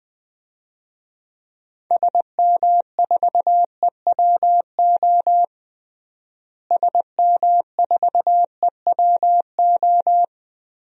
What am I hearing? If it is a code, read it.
Morse code, heard as a keyed tone, about 20 wpm, 700 Hz. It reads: SM4EWO SM4EWO
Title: DXCC-100 Call Signs - 6 Characters - Repeated 3x - 20wpm